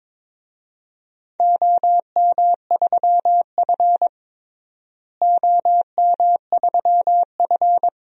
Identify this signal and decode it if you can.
Morse code, heard as a keyed tone, about 22 wpm, 700 Hz. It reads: OM3F OM3F
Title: DXCC-100 Call Signs - 4 Characters - Repeated 2x - 22wpm